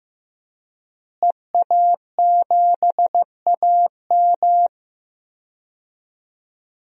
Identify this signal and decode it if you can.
Morse code, heard as a keyed tone, about 15 wpm, 700 Hz. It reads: EA7AM
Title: DXCC-100 Call Signs - 5 Characters - Repeated 2x - 15wpm